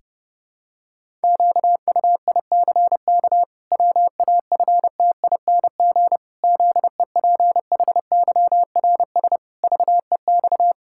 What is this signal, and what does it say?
Morse code, heard as a keyed tone, about 30 wpm, 700 Hz. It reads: QUICK WAFTING ZEPHYRS VEX
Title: Pangram Challenge v2 - 30wpm